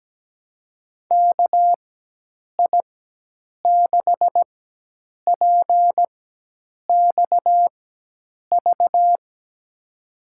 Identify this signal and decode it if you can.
Morse code, heard as a keyed tone, about 17 wpm, spaced as for 8 wpm, 700 Hz. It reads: KI6PXV
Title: USA Call Signs - 6 Characters - 4x Character Spacing - 17wpm